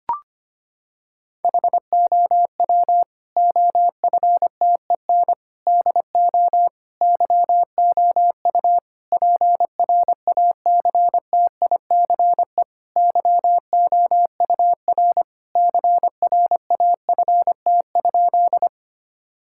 Morse code, heard as a keyed tone, about 25 wpm, 700 Hz. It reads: HOW OFTEN DO YOU PRACTICE YOUR CRAFT?
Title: Sentences from Top 1000 Words - Review Part 6 - 25wpm